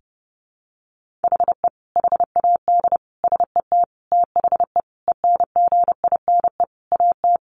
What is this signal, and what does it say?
Morse code, heard as a keyed tone, about 30 wpm, 700 Hz. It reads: HE HAD SET THE ENGINE AT
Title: Sentences from Top 1500 Words 30wpm